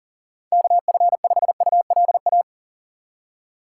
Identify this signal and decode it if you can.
Morse code, heard as a keyed tone, about 40 wpm, 700 Hz. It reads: KF5ULA
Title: USA Call Signs - 6 Characters - 6x Character Spacing - 40wpm